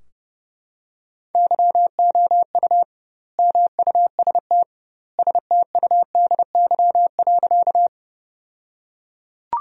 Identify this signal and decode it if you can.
Morse code, heard as a keyed tone, about 30 wpm, 700 Hz. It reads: YOU MUST STUDY.
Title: 2x Word-Spacing - Sentences from Top 300 Words - 30wpm